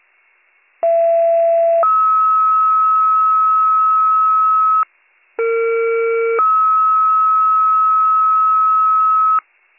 Two-tone page, heard as a one-second tone, then a three-second tone, about 670 then 1290 Hz, about 460 then 1290 Hz. It is heard from 0.8 to 9.4 s.